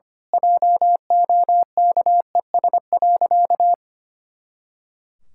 Morse code, heard as a keyed tone, about 25 wpm, 700 Hz. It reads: JOKES.
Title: Pangram Challenge v2 - 25wpm